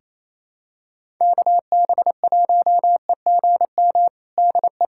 Morse code, heard as a keyed tone, about 28 wpm, 700 Hz. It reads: KB1EGM DE
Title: USA Callsign DE Callsign 28wpm